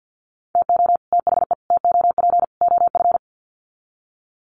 Morse code, heard as a keyed tone, about 50 wpm, 700 Hz. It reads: TO THE TOP OF
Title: Sets of 4 Words 50wpm